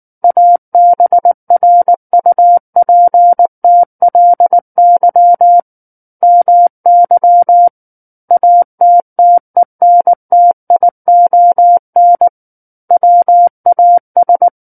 Morse code, heard as a keyed tone, about 19 wpm, 700 Hz. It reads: ABRUPTLY MY ATTENTION WAS